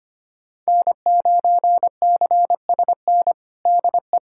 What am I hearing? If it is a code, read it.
Morse code, heard as a keyed tone, about 25 wpm, 700 Hz. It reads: N9CSN DE
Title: USA Callsign DE Callsign 25wpm